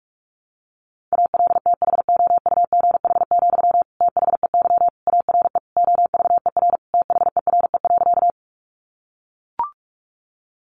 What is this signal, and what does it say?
Morse code, heard as a keyed tone, about 44 wpm, 700 Hz. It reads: ALTHOUGH, THEY ARE OVER THERE.
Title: Sentences from Top 600 Words 45wpm